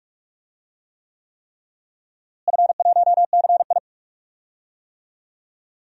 Morse code, heard as a keyed tone, about 46 wpm, 700 Hz. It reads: F1CI
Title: DXCC-100 Call Signs - 4 Characters - Repeated 3x - 45wpm